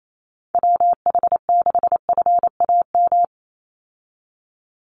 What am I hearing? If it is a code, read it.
Morse code, heard as a keyed tone, about 28 wpm, 700 Hz. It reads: WH6FAM
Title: USA Call Signs 28wpm